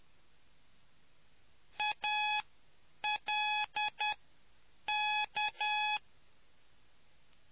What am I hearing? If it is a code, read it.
Morse code, heard as a keyed tone, about 10 wpm, spaced as for 7 wpm, 840 Hz. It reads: ALK